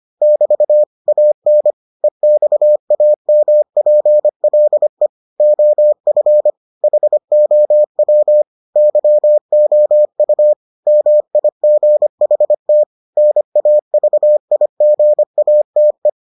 Morse code, heard as a keyed tone, about 25 wpm, 600 Hz. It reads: <BT> AN EXAMPLE OF HOW YOU MIGHT NAVIGATE